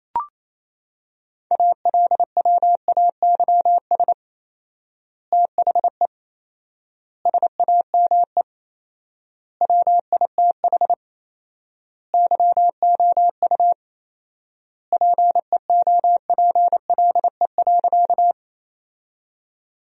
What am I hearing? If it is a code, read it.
Morse code, heard as a keyed tone, about 28 wpm, 700 Hz. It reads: ALWAYS THE SAME WITH YOU PEOPLE.
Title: 4x Word-Spacing - Sentences from Top 300 Words - 28wpm